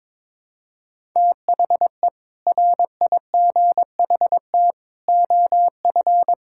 Morse code, heard as a keyed tone, about 22 wpm, 700 Hz. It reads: THE RIGHT OF
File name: practice / 22wpm / Sentences with 8-Word Phrases - 22wpm